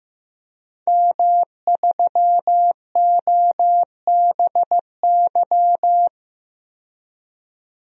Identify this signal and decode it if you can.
Morse code, heard as a keyed tone, about 15 wpm, 700 Hz. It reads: M3OBY
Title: DXCC-100 Call Signs - 5 Characters - 15wpm